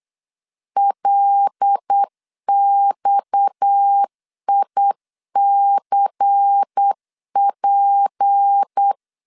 Morse code, heard as a keyed tone, about 8 wpm, 800 Hz. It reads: LXICP
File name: 8wpm